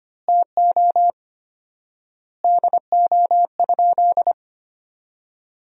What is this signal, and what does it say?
Morse code, heard as a keyed tone, about 25 wpm, 700 Hz. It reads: TO DO?
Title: 4x Word-Spacing - Sentences from Top 200 Words - 25wpm